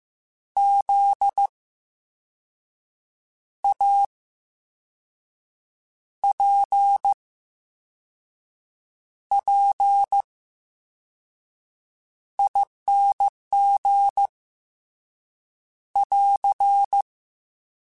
Morse code, heard as a keyed tone, about 15 wpm, spaced as for 5 wpm, 800 Hz. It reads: ZAPPING <AR>
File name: Special INGs-mp3s / Special INGs-mp3s / VerySlowToFaster Ing @13